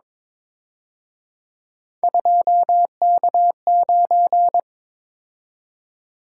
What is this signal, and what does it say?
Morse code, heard as a keyed tone, about 25 wpm, 700 Hz. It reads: 2K9
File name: Speed-Racing - Three Letter-Number Combinations - 22wpm